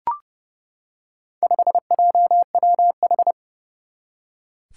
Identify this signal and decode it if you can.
Morse code, heard as a keyed tone, about 32 wpm, 700 Hz. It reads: 5JWH
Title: Four Challenging Letter-Number Combinations 30wpm